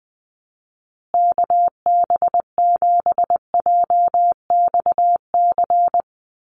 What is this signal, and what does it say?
Morse code, heard as a keyed tone, about 19 wpm, 700 Hz. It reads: KB7JXC
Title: USA Call Signs Repeated 3x 20wpm